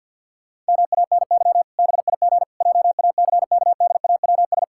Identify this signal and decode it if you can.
Morse code, heard as a keyed tone, about 50 wpm, 700 Hz. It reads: MANY BIG JACKDAWS
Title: Pangram Challenge v2 - 50wpm